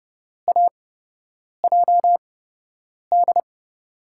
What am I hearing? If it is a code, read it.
Morse code, heard as a keyed tone, about 30 wpm, spaced as for 8 wpm, 700 Hz. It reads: AJD
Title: USA Call Signs - 6 Characters - 8x Character Spacing - 30wpm